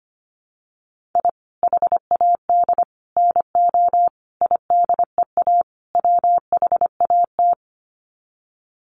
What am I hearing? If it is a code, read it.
Morse code, heard as a keyed tone, about 25 wpm, 700 Hz. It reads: I HAD NO IDEA WHAT
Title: Sets of 5 Words Encore 25wpm